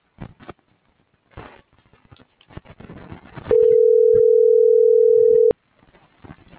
Ringback tone (North America), 440 + 480 Hz.